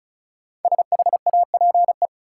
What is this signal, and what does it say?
Morse code, heard as a keyed tone, about 36 wpm, 700 Hz. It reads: SHAPE
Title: Sentences from Top 500 Words - New Word Drill - 35wpm